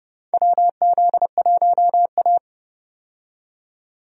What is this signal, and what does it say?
Morse code, heard as a keyed tone, about 30 wpm, 700 Hz. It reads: WZ1A
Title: USA Call Signs - 4 Characters - 30wpm